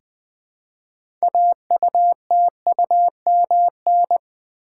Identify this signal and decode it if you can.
Morse code, heard as a keyed tone, about 20 wpm, 700 Hz. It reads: AUTUMN
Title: 6 Letter Words - First 3 Letters Given - 20wpm